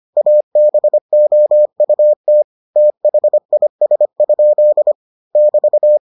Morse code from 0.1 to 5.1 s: ABOUT THIS?

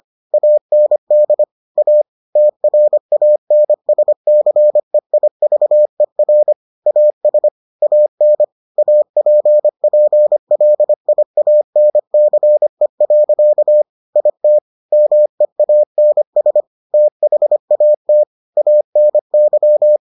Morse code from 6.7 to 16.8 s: AS AN APPLIANCE. IT MEANS

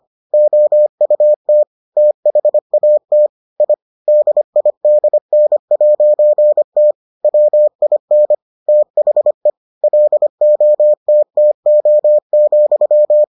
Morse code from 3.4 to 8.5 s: I DIDN'T WIN

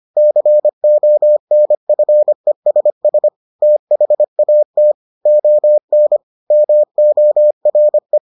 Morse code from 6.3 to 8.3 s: MORE

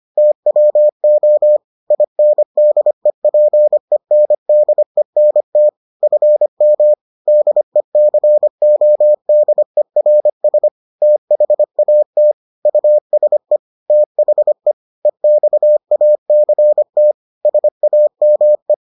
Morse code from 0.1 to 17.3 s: TWO INDEPENDENT FM DECODERS THAT USE THE EXACT

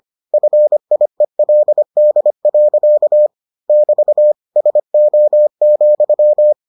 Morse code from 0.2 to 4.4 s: FIELD. <BT>

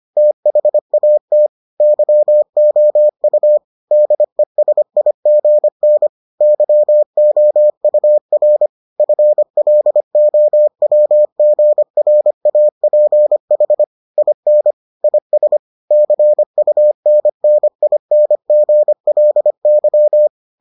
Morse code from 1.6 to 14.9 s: YOU DESIGN YOUR FLOWGRAPH IN

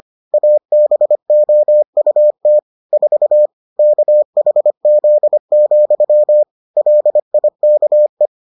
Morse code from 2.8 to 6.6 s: 4 KHZ,

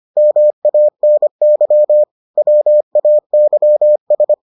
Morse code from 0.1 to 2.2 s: MANY

